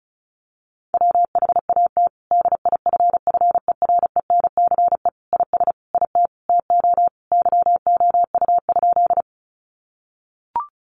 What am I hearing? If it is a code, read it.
Morse code, heard as a keyed tone, about 34 wpm, 700 Hz. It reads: WHAT DIFFERENCE IS IT TO YOU?
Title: Sentences from Top 800 Words 35wpm